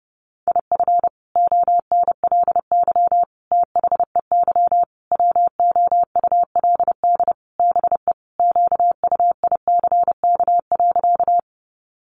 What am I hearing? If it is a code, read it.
Morse code, heard as a keyed tone, about 28 wpm, 700 Hz. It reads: IF ONLY THEY WOULD BE QUICK.
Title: Sentences from Top 500 Words 30wpm